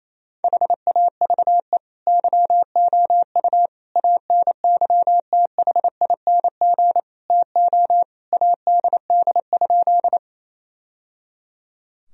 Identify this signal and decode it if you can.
Morse code, heard as a keyed tone, about 28 wpm, 700 Hz. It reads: HAVE YOU ANYTHING TO ADD?
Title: Sentences from Top 1000 Words - Review Part 1 - 28wpm